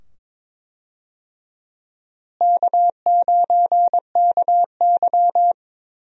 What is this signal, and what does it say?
Morse code, heard as a keyed tone, about 22 wpm, 700 Hz. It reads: K9KY
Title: USA Call Signs - 4 Characters - Repeated 2x - 22wpm